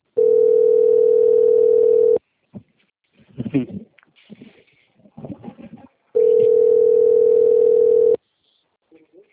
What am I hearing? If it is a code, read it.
Ringback tone (North America), 440 + 480 Hz.
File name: out- - - - - .6768